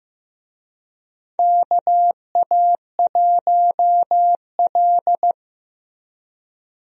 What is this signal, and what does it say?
Morse code, heard as a keyed tone, about 15 wpm, 700 Hz. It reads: KA1L
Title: USA Call Signs - 4 Characters - Repeated 2x - 15wpm